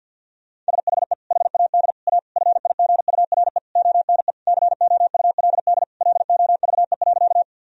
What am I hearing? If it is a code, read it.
Morse code, heard as a keyed tone, about 50 wpm, 700 Hz. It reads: SHE HAD A FIGURE ONE COULD LOVE.
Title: Sentences from Top 1000 Words - Review Part 2 - 50wpm